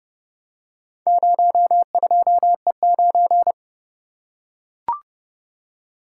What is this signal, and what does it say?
Morse code, heard as a keyed tone, about 30 wpm, 700 Hz. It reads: 02E9